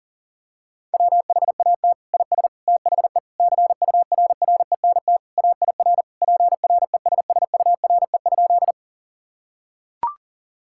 Morse code, heard as a keyed tone, about 40 wpm, 700 Hz. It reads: WHAT IS THE CURRENT AIR PRESSURE?
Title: Sentences from Top 1000 Words - Review Part 6 - 40wpm